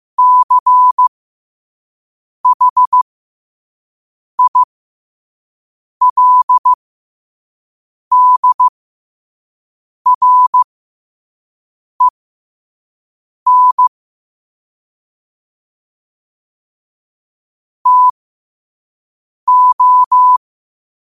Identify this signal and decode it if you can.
Morse code, heard as a keyed tone, about 15 wpm, spaced as for 5 wpm, 1000 Hz. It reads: CHILDREN TO
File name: qotdfiles / 5wpm